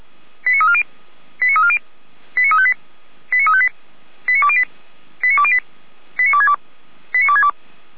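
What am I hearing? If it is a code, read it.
Five-tone selective call (ZVEI), heard as quick runs of five tones, tones of about 70 ms. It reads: 89340 89340 89358 89358 89208 89208 89262 89262
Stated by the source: ZVEI five-tone call received on SDR.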